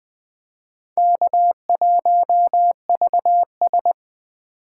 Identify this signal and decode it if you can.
Morse code, heard as a keyed tone, about 21 wpm, 700 Hz. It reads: K1VS
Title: Speed Racing - USA 1x2 Call Signs 17wpm